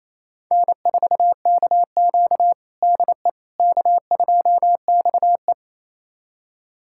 Morse code, heard as a keyed tone, about 27 wpm, 700 Hz. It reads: N4KQ DE K2XE
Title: USA Callsign DE Callsign - 4 Characters - 28wpm